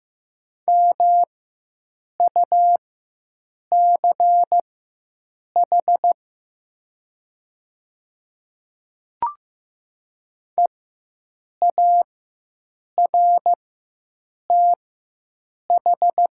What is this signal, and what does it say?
Morse code, heard as a keyed tone, about 15 wpm, spaced as for 7 wpm, 700 Hz. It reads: MUCH EARTH
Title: 4x Character Spacing - Top 200 Words - 15wpm